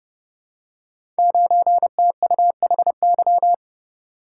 Morse code, heard as a keyed tone, about 31 wpm, 700 Hz. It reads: 9TUHY
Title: Code Groups - 5 Letter:Numbers - 30wpm